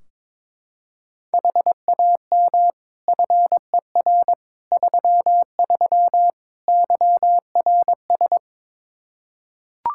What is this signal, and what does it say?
Morse code, heard as a keyed tone, about 21 wpm, 700 Hz. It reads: HAM FER 33 YRS
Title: Practice QSO - Exchange 3 - More Info - 22wpm